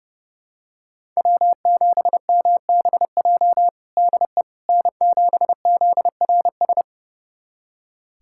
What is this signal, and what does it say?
Morse code, heard as a keyed tone, about 32 wpm, 700 Hz. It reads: W7MBJ DE N7ZRS
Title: USA Callsign DE Callsign - 5 Characters - 30wpm